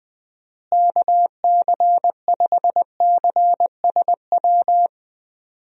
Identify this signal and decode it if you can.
Morse code, heard as a keyed tone, about 21 wpm, 700 Hz. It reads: KC5CSW
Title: USA Call Signs - 6 Characters - Repeated 3x - 20wpm